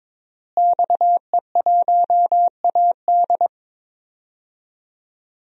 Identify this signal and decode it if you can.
Morse code, heard as a keyed tone, about 21 wpm, 700 Hz. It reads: XE1AD
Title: DXCC-100 Call Signs - 5 Characters - Repeated 2x - 22wpm